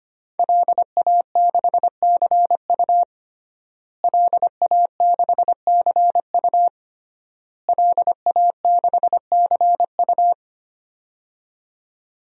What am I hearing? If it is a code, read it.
Morse code, heard as a keyed tone, about 25 wpm, 700 Hz. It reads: LA6CU LA6CU LA6CU
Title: DXCC-100 Call Signs - 5 Characters - Repeated 3x - 25wpm